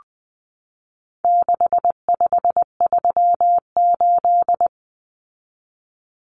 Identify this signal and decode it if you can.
Morse code, heard as a keyed tone, about 20 wpm, 700 Hz. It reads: 6538
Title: Common Cut Numbers - 4-Digit 20wpm